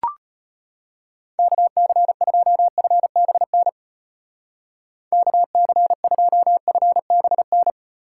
Morse code, heard as a keyed tone, about 30 wpm, 700 Hz. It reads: KC2FBN KC2FBN
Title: Speed Racing - USA 2x3 Call Signs 25wpm